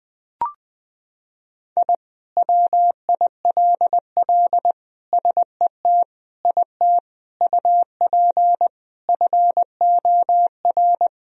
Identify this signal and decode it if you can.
Morse code, heard as a keyed tone, about 21 wpm, 700 Hz. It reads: I WILL SET IT UP FOR
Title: Sentences from Top 1000 Words - Review Part 1 - 20wpm